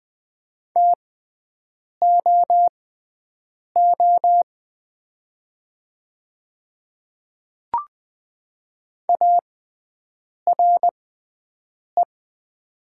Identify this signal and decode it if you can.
Morse code, heard as a keyed tone, about 20 wpm, spaced as for 7 wpm, 700 Hz. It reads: TOO ARE
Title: 6x Character Spacing - Top 200 Words - 20wpm